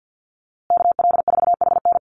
Morse code, heard as a keyed tone, about 49 wpm, 700 Hz. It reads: KL4HN
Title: USA Call Signs Repeated 3x 50wpm